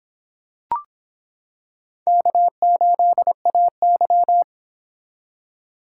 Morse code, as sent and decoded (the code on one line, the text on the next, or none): -.- ---.. .- -.--
K8AY